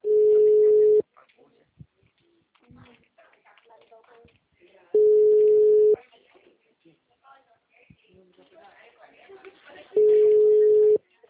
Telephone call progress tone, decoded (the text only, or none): ringback tone